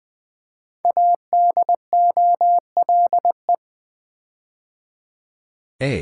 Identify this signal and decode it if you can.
Morse code, heard as a keyed tone, about 20 wpm, 700 Hz. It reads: ADOLE
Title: Build A Word - 5 Letters At A Time - 10-Letter Words - 20wpm